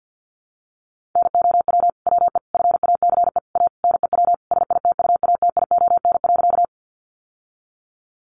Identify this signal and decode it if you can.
Morse code, heard as a keyed tone, about 50 wpm, 700 Hz. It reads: NOW WE FACE A NEW SITUATION.
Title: Sentences from Top 1200 Words 50wpm